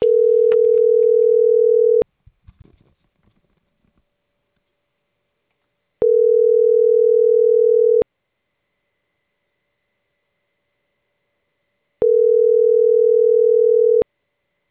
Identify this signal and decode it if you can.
Ringback tone (North America), 440 + 480 Hz.